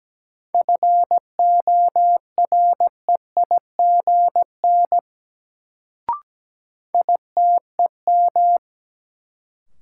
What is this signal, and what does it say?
Morse code, heard as a keyed tone, about 17 wpm, 700 Hz. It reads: FOREIGN ITEM
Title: Sentences from Top 1400 Words - New Word Drill Alt - 17wpm